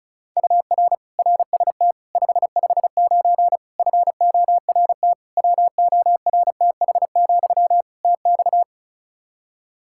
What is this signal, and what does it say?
Morse code, heard as a keyed tone, about 36 wpm, 700 Hz. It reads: UR RST 559 FORT WORTH, TX
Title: Practice QSO - Exchange 1 - Beginning - 35wpm